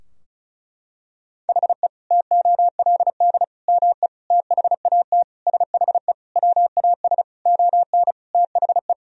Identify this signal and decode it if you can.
Morse code, heard as a keyed tone, about 34 wpm, 700 Hz. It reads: HE TOLD ME THAT SHE WAS ON THE
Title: Sentences from Top 1000 Words - Review Part 2 - 35wpm